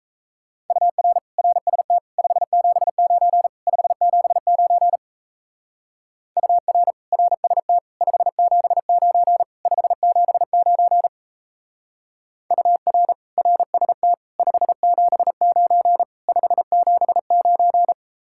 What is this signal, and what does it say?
Morse code, heard as a keyed tone, about 34 wpm, 700 Hz. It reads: UR RST 579 579 UR RST 579 579 UR RST 579 579
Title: Speed-Racing Common QSO Phrases - 28wpm